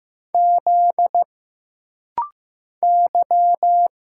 Morse code, heard as a keyed tone, about 15 wpm, 700 Hz. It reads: Z Y
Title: Single Letter-Number Alt - Rapid-Fire 15wpm